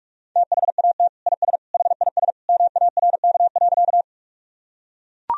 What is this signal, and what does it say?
Morse code, heard as a keyed tone, about 43 wpm, 700 Hz. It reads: THAT IS HIS MARK.
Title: Sentences from Top 1000 Words - Review Part 1 - 45wpm